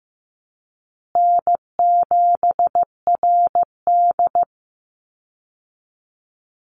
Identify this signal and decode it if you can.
Morse code, heard as a keyed tone, about 15 wpm, 700 Hz. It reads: N7RD
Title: Easy Call Signs Repeated 3x - 15wpm